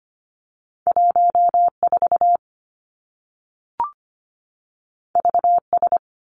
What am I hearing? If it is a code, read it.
Morse code, heard as a keyed tone, about 25 wpm, 700 Hz. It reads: 1 4 V S
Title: Two Challenging Letter-Number Combinations 25wpm